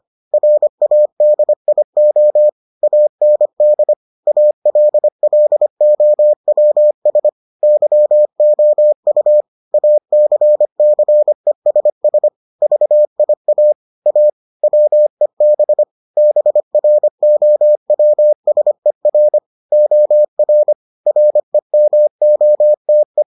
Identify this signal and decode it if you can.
Morse code, heard as a keyed tone, about 25 wpm, 600 Hz. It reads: RADIO AND ALLOWS YOU ACCESS VIA A WEB BROWSER OR REMOTE